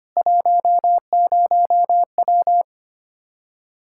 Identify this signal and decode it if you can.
Morse code, heard as a keyed tone, about 25 wpm, 700 Hz. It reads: 10W
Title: Practice QSO - Exchange 2 - Info - 25wpm